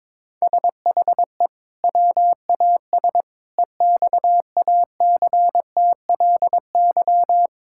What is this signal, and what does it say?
Morse code, heard as a keyed tone, about 22 wpm, 700 Hz. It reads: SHE WAS EXACTLY
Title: Sentences from Top 1000 Words - Review Part 6 - 22wpm